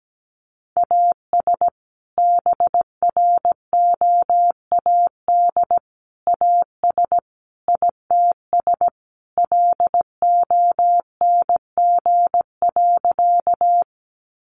Morse code, heard as a keyed tone, about 17 wpm, 700 Hz. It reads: AS BROAD AS ITS LONG.
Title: Sentences from Top 1000 Words 17wpm